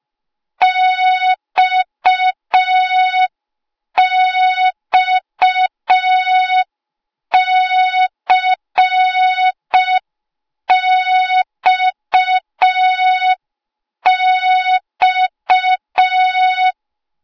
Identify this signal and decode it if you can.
Morse code, heard as a keyed tone, about 5 wpm, 3700 Hz. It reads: XXCXX
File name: Morze-HA2MN / hang / 08KXC HA2MN